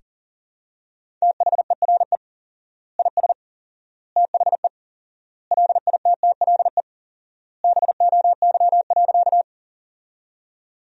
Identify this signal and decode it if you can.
Morse code, heard as a keyed tone, about 40 wpm, 700 Hz. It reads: THERE IS THE LITTLE BOY.
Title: 4x Word-Spacing - Sentences from Top 200 Words - 40wpm